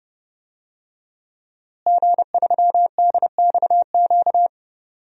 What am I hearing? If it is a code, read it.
Morse code, heard as a keyed tone, about 30 wpm, 700 Hz. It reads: G3DXQ
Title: DXCC-100 Call Signs - 5 Characters - Repeated 3x - 30wpm